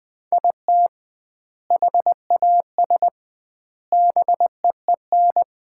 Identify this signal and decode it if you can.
Morse code, heard as a keyed tone, about 20 wpm, 700 Hz. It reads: IT HAS BEEN
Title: 2x Word-Spacing - Sets of 3 Words - 20wpm